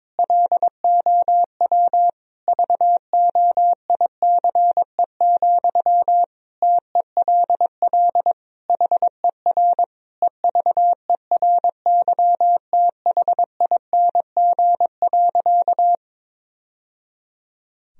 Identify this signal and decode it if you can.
Morse code, heard as a keyed tone, about 22 wpm, 700 Hz. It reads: LOW VOICE, TELL HER EVERYTHING.
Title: Sentences from Top 1000 Words - Review Part 1 - 22wpm